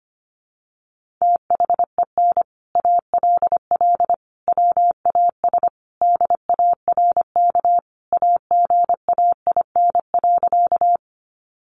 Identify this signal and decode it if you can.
Morse code, heard as a keyed tone, about 25 wpm, 700 Hz. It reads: THEN ALL WAS DARK AGAIN.
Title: Sentences from Top 500 Words 25wpm